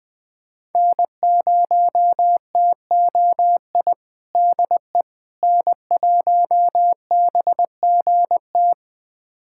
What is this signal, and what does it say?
Morse code, heard as a keyed tone, about 20 wpm, 700 Hz. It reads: N0TOI DE N1BGT
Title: USA Callsign DE Callsign - 5 Characters - 20wpm